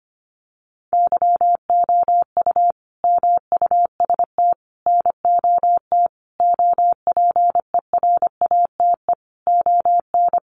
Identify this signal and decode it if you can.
Morse code, heard as a keyed tone, about 25 wpm, 700 Hz. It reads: YOU MUST NOT OPERATE ON
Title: Sentences from Top 1500 Words 25wpm